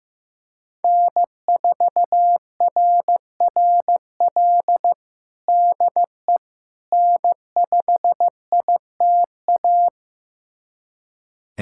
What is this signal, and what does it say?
Morse code, heard as a keyed tone, about 15 wpm, 700 Hz. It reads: N4RRL DE N5ITA
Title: USA Callsign DE Callsign - 5 Characters - 15wpm